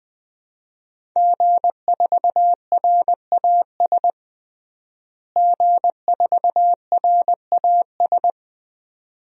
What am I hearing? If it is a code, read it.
Morse code, heard as a keyed tone, about 20 wpm, 700 Hz. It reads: G4RAS G4RAS
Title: DXCC-100 Call Signs - 5 Characters - Repeated 2x - 20wpm